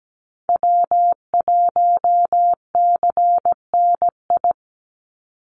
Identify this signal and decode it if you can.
Morse code, heard as a keyed tone, about 17 wpm, 700 Hz. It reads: W1CNI
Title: USA Call Signs 17wpm